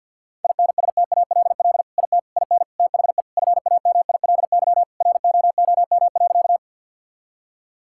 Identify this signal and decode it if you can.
Morse code, heard as a keyed tone, about 50 wpm, 700 Hz. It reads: INSTALL IT IN THE FAMILY ROOM.
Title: Sentences from Top 1600 Words 50wpm